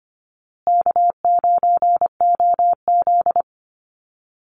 Morse code, heard as a keyed tone, about 25 wpm, 700 Hz. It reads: K9OZ
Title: Call Signs Repeated 3x - 25wpm